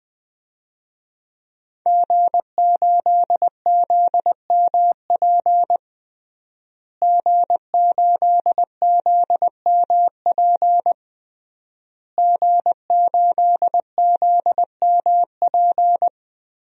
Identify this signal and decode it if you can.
Morse code, heard as a keyed tone, about 20 wpm, 700 Hz. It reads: G8ZMP G8ZMP G8ZMP
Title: DXCC-100 Call Signs - 5 Characters - Repeated 3x - 20wpm